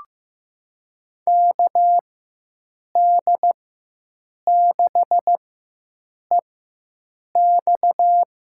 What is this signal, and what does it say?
Morse code, heard as a keyed tone, about 15 wpm, spaced as for 7 wpm, 700 Hz. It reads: KD6EX
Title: USA Call Signs - 4x Character Spacing - 15wpm